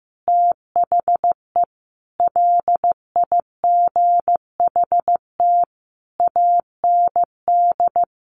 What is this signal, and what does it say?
Morse code, heard as a keyed tone, about 15 wpm, 700 Hz. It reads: THE LIGHT AND
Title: Sentences from Top 1300 Words 15wpm